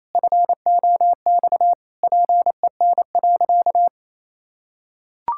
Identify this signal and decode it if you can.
Morse code, heard as a keyed tone, about 28 wpm, 700 Hz. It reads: FOX PEN.
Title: Pangram Challenge v2 - 28wpm